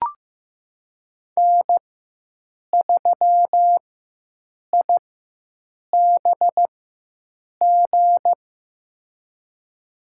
Morse code, heard as a keyed tone, about 15 wpm, spaced as for 7 wpm, 700 Hz. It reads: N3IBG
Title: USA Call Signs - 5 Characters - 4x Character Spacing - 15wpm